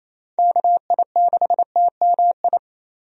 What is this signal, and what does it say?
Morse code, heard as a keyed tone, about 28 wpm, 700 Hz. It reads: KI6TMI
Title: USA Call Signs - 6 Characters - Repeated 3x - 28wpm